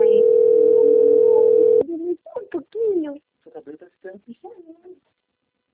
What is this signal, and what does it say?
Ringback tone (North America), 440 + 480 Hz, with speech.
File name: rg-1600-unknown- - - .22522